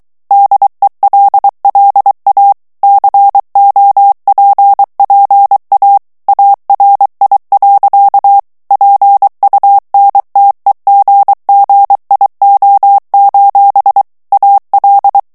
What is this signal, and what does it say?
Morse code, heard as a keyed tone, about 23 wpm, 800 Hz. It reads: DELLA COPPA ARI. PUNTEGGIO: AL
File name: ITALIEN texte clair 2 vitesse 22 mots minute